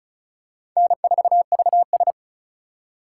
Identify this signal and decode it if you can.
Morse code, heard as a keyed tone, about 36 wpm, 700 Hz. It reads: N4VS
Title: USA Call Signs - 4 Characters - 2x Character Spacing - 35wpm